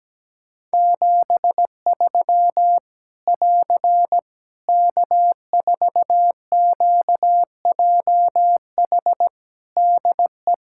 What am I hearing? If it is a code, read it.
Morse code, heard as a keyed tone, about 17 wpm, 700 Hz. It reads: 73 <AR> K4QJH DE
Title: Practice QSO - Exchange 4 - Ending - 17wpm